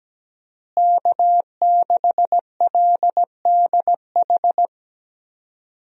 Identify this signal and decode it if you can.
Morse code, heard as a keyed tone, about 17 wpm, 700 Hz. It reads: K6LDH
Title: USA Call Signs - 5 Characters - Repeated 2x - 17wpm